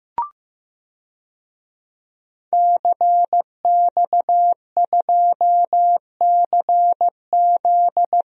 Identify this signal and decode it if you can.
Morse code, heard as a keyed tone, about 15 wpm, 700 Hz. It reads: CX2CZ
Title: DXCC-100 Call Signs - 5 Characters - Repeated 3x - 15wpm